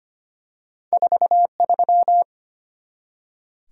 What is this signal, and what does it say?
Morse code, heard as a keyed tone, about 25 wpm, 700 Hz. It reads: 43
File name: Two Letter-Number Combinations - 25wpm